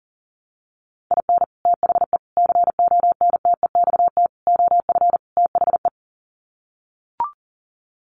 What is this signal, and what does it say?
Morse code, heard as a keyed tone, about 40 wpm, 700 Hz. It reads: IN THE CONTEXT OF THE
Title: Sets of 5 Words 40wpm